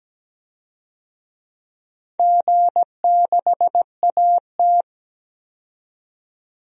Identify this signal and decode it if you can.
Morse code, heard as a keyed tone, about 17 wpm, 700 Hz. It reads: G6AT
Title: DXCC-100 Call Signs - 4 Characters - 17wpm